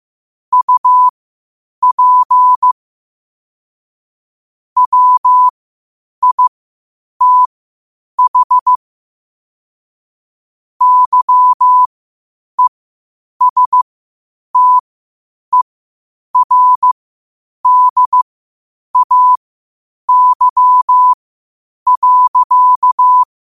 Morse code, heard as a keyed tone, about 15 wpm, spaced as for 8 wpm, 1000 Hz. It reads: UP WITH YESTERDAY.
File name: qotdfiles / 8wpm